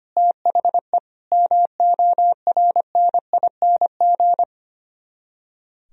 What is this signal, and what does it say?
Morse code, heard as a keyed tone, about 31 wpm, 700 Hz. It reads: THE MORNING
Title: Speed-Racing - Sets of 3 Words 25wpm